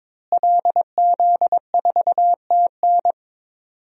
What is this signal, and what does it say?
Morse code, heard as a keyed tone, about 22 wpm, 700 Hz. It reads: LZ4TN
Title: DXCC-100 Call Signs - 5 Characters - Repeated 3x - 22wpm